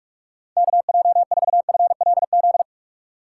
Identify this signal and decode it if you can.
Morse code, heard as a keyed tone, about 47 wpm, 700 Hz. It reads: KJ4FLZ
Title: USA Call Signs - 6 Characters - 2x Character Spacing - 45wpm